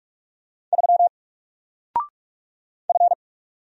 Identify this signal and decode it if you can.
Morse code, heard as a keyed tone, about 44 wpm, 700 Hz. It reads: 3 F
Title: Single Letter-Number Alt 45wpm